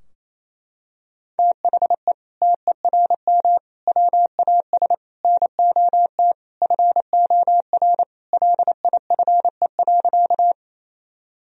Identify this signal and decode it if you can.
Morse code, heard as a keyed tone, about 27 wpm, 700 Hz. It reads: THE TERM WAS NOT FOR LIFE.
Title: Sentences from Top 1000 Words - Review Part 4 - 28wpm